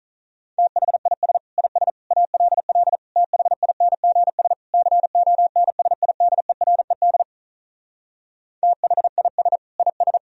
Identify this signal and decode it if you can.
Morse code, heard as a keyed tone, about 46 wpm, 700 Hz. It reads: THIS IS ALL THINGS CONSIDERED THIS IS